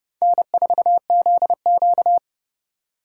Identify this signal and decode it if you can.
Morse code, heard as a keyed tone, about 36 wpm, 700 Hz. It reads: N4ZQ
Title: Speed Racing - USA 1x2 Call Signs 30wpm